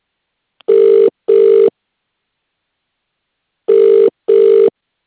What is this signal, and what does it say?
Ringback tone (UK), 400 + 450 Hz.